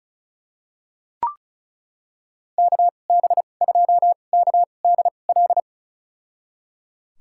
Morse code, heard as a keyed tone, about 36 wpm, spaced as for 25 wpm, 700 Hz. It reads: KB2KDL
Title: USA Call Signs - 6 Characters - 2x Character Spacing - 35wpm